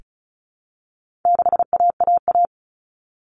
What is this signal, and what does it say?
Morse code, heard as a keyed tone, about 34 wpm, 700 Hz. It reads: 6AAA
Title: Common Cut Numbers - 4-Digit 35wpm